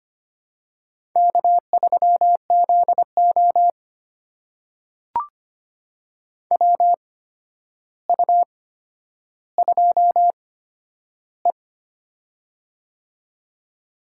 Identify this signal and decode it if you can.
Morse code, heard as a keyed tone, about 25 wpm, spaced as for 11 wpm, 700 Hz. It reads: K3ZO WU2E
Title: USA Call Signs - 4 Characters - 8x Character Spacing - 25wpm